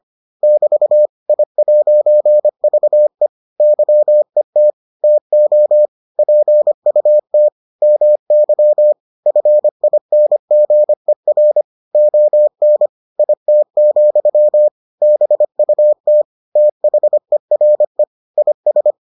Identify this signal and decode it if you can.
Morse code, heard as a keyed tone, about 25 wpm, 600 Hz. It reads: <BT> I'VE YET TO PUT MY FINGER ON IT, BUT THERE IS